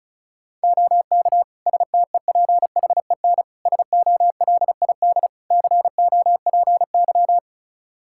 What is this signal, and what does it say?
Morse code, heard as a keyed tone, about 34 wpm, 700 Hz. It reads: OK STEPHEN SOLID COPY
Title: K1USN SST QSO Practice 35wpm